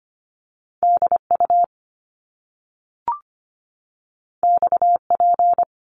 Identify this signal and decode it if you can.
Morse code, heard as a keyed tone, about 25 wpm, 700 Hz. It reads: DU XP
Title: Top Two Letter Combinations in Words 25wpm